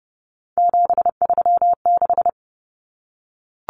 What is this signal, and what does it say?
Morse code, heard as a keyed tone, about 30 wpm, 700 Hz. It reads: 736
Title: Common Cut Numbers - 3-Digit 30wpm